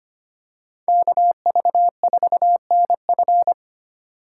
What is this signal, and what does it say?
Morse code, heard as a keyed tone, about 25 wpm, 700 Hz. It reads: KV4NF
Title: USA Call Signs - 5 Characters - 6x Character Spacing - 25wpm